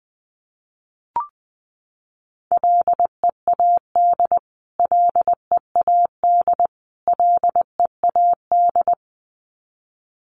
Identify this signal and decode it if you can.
Morse code, heard as a keyed tone, about 20 wpm, 700 Hz. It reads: LEAD LEAD LEAD
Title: Top 500 Words Repeated 20wpm